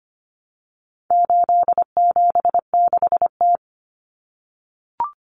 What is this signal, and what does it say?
Morse code, heard as a keyed tone, about 25 wpm, 700 Hz. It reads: 876T
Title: Common Cut Numbers - 4-Digit 25wpm